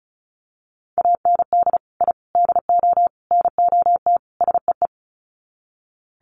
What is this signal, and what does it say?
Morse code, heard as a keyed tone, about 34 wpm, 700 Hz. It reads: AND I DO NOT SEE
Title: Sets of 5 Words Encore2 35wpm